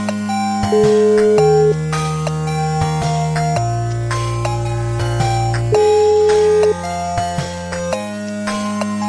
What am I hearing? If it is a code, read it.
Ringback tone (Europe, 425 Hz), with noise.